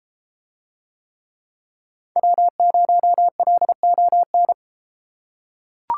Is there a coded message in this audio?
Yes: Morse code from 2.2 to 4.5 s, heard as a keyed tone, about 33 wpm, 700 Hz. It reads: W0LON